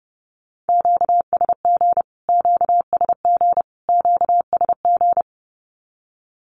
Morse code, heard as a keyed tone, about 30 wpm, 700 Hz. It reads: QSG QSG QSG
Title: Q-Codes Repeated 3x 30wpm